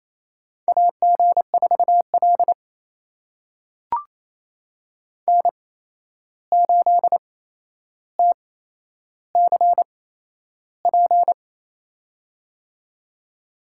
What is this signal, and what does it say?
Morse code, heard as a keyed tone, about 28 wpm, spaced as for 11 wpm, 700 Hz. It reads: AG4L N8TCP
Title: USA Call Signs - 8x Character Spacing - 28wpm